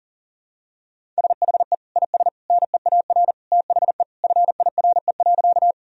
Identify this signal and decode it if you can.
Morse code, heard as a keyed tone, about 40 wpm, 700 Hz. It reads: SHE IS NEAR THE FIRE.
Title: Sentences from Top 1000 Words - Review Part 1 - 40wpm